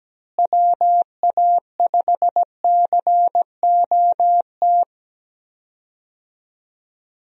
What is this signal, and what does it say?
Morse code, heard as a keyed tone, about 17 wpm, 700 Hz. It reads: WA5COT
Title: USA Call Signs - 6 Characters - Repeated 3x - 17wpm